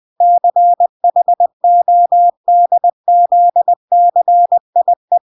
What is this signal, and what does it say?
Morse code, heard as a keyed tone, about 20 wpm, 700 Hz. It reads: CHODZCIE
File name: recordings / 20wpm0000